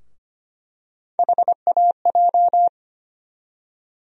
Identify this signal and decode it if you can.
Morse code, heard as a keyed tone, about 25 wpm, 700 Hz. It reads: HAJ
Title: USA Call Sign Postfixes - 3 Letters 25wpm